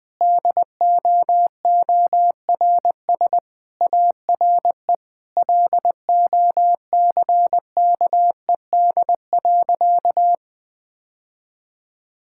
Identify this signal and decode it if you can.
Morse code, heard as a keyed tone, about 20 wpm, 700 Hz. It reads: DOORS ARE LOCKED.
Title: Sentences with 5-Word Phrases - 20wpm